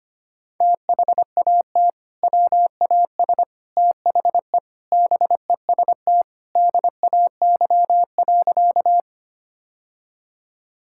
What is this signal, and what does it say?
Morse code, heard as a keyed tone, about 25 wpm, 700 Hz. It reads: THAT WAS THE BEST DAY.
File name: practice / 25wpm / Sentences with 3-Word Phrases - 25wpm